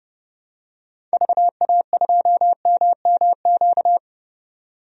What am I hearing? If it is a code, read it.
Morse code, heard as a keyed tone, about 30 wpm, 700 Hz. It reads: VA2MMQ
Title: DXCC-100 Call Signs - 6 Characters - 30wpm